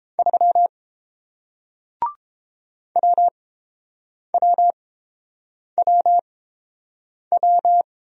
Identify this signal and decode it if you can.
Morse code, heard as a keyed tone, about 30 wpm, spaced as for 7 wpm, 700 Hz. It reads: 3 WWWW